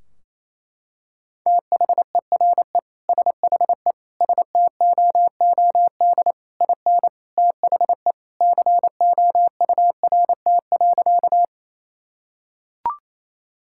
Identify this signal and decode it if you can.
Morse code, heard as a keyed tone, about 28 wpm, 700 Hz. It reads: THERE SHE STOOD IN THE COURT.
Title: Sentences from Top 1600 Words 28wpm